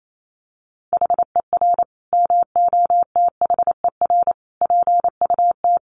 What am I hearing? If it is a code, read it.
Morse code, heard as a keyed tone, about 28 wpm, 700 Hz. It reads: HER MOTHER PUT
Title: Sentences from Top 800 Words 28wpm